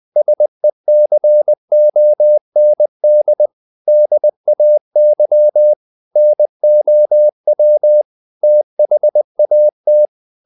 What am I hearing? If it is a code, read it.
Morse code, heard as a keyed tone, about 20 wpm, 600 Hz. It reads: SECOND DAY NOW THAT
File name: the old man and the sea / the old man and the sea part 1 20wpm